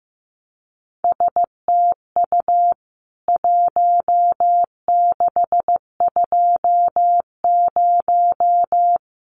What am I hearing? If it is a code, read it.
Morse code, heard as a keyed tone, about 15 wpm, 700 Hz. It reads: STU 1620
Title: CWT Exchange 15wpm